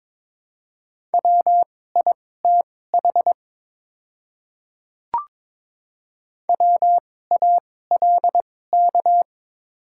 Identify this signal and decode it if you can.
Morse code, heard as a keyed tone, about 21 wpm, spaced as for 16 wpm, 700 Hz. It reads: WITH WALK
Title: 2x Character Spacing - Top 500 Words - 22wpm